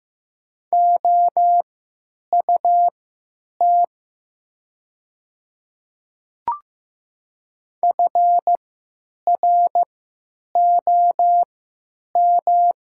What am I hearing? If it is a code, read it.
Morse code, heard as a keyed tone, about 15 wpm, spaced as for 9 wpm, 700 Hz. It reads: OUT FROM
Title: 3x Character Spacing - Top 100 Words - 15wpm